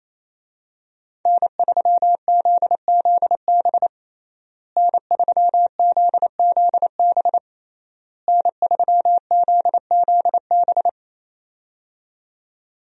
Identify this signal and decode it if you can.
Morse code, heard as a keyed tone, about 27 wpm, 700 Hz. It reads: N3ZZB N3ZZB N3ZZB
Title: USA Call Signs - 5 Characters - Repeated 3x - 28wpm